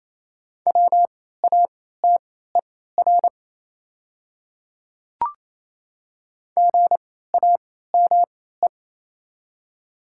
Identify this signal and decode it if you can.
Morse code, heard as a keyed tone, about 27 wpm, spaced as for 16 wpm, 690 Hz. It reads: WATER GAME
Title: 3x Character Spacing - Top 500 Words - 28wpm